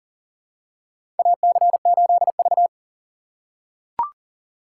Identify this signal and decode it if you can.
Morse code, heard as a keyed tone, about 40 wpm, 700 Hz. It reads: AC8V
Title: USA Call Signs - 4 Characters - 6x Character Spacing - 40wpm